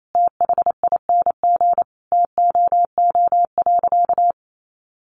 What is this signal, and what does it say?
Morse code, heard as a keyed tone, about 28 wpm, 700 Hz. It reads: THING TOO.
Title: Sentences from Top 500 Words 28wpm